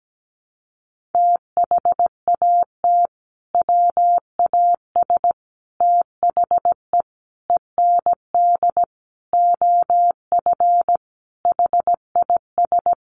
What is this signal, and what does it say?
Morse code, heard as a keyed tone, about 17 wpm, 700 Hz. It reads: THAT WAS THE END OF HIS
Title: Sentences from Top 1400 Words 17wpm